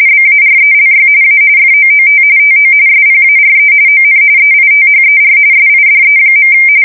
RTTY, about 45.45 baud. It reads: BARELY A FOOT TO SPARE ON EITHER SIDE.
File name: fsk170 / battleshipnj